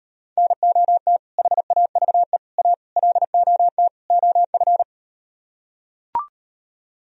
Morse code, heard as a keyed tone, about 38 wpm, 700 Hz. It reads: NOT HAVE A LOT OF